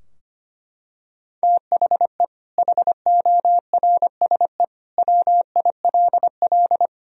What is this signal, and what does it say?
Morse code, heard as a keyed tone, about 25 wpm, 700 Hz. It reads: THE HORSE WILL
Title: Sentences from Top 1000 Words - Review Part 2 - 25wpm